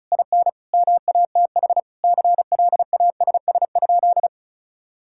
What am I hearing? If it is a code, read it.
Morse code, heard as a keyed tone, about 36 wpm, 700 Hz. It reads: IN MATH CLASS?
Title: Sentences from Top 1000 Words - Review Part 6 - 35wpm